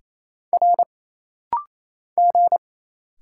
Morse code, heard as a keyed tone, about 28 wpm, 700 Hz. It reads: R G
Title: Single Letter-Number - Rapid-Fire 28wpm